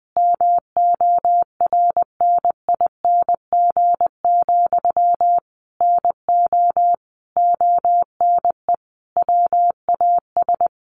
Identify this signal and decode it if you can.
Morse code, heard as a keyed tone, about 20 wpm, 700 Hz. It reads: MORNING, NO ONE WAS
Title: Sentences from Top 1400 Words 20wpm